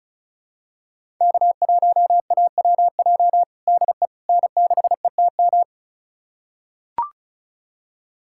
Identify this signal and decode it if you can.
Morse code, heard as a keyed tone, about 36 wpm, 700 Hz. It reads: K1AWJ DE N6ETM
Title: USA Callsign DE Callsign 35wpm